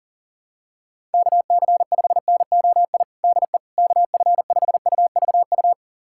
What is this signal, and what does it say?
Morse code, heard as a keyed tone, about 40 wpm, 700 Hz. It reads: KC5NOI DE KF5UVU
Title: USA Callsign DE Callsign - 6 Characters - 40wpm